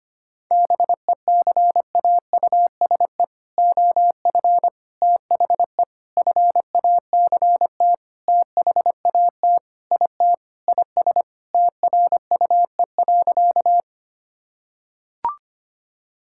Morse code, heard as a keyed tone, about 25 wpm, 700 Hz. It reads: BECAUSE OF THE FACT THAT IT IS TRUE.
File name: practice / 25wpm / Sentences with 4-Word Phrases - 25wpm